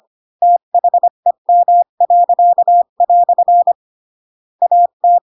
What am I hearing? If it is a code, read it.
Morse code, heard as a keyed tone, about 25 wpm, 700 Hz. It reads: THEM." AT
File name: Pinocchio / Pinocchio ch7 20 25 ik1hge